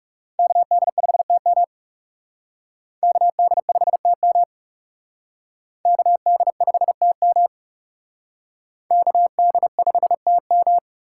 Morse code, heard as a keyed tone, about 38 wpm, 700 Hz. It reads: KD5TM KD5TM KD5TM KD5TM